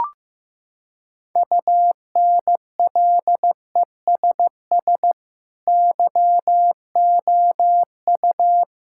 Morse code, heard as a keyed tone, about 15 wpm, 700 Hz. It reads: UNLESS YOU
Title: Sentences from Top 1000 Words - Review Part 5 - 15wpm